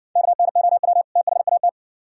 Morse code, heard as a keyed tone, about 51 wpm, 700 Hz. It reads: KNOW THAT